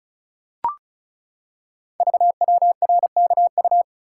Morse code, heard as a keyed tone, about 36 wpm, 700 Hz. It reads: VWRKU
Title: Code Groups - 5 Letter:Numbers - 35wpm